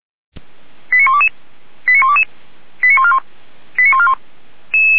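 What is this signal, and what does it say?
Five-tone selective call (ZVEI), heard as quick runs of five tones, tones of about 80 ms. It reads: 89122 89122 89141 89141
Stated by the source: ZVEI five-tone call received by SDR.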